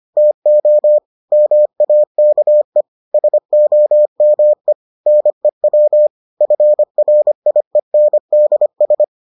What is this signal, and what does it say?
Morse code, heard as a keyed tone, about 25 wpm, 600 Hz. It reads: TO MAKE SOME NEW FRIENDS